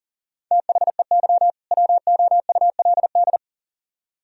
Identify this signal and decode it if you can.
Morse code, heard as a keyed tone, about 49 wpm, 700 Hz. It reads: THEY WOULD
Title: Speed-Racing - Sets of 2 Words 40wpm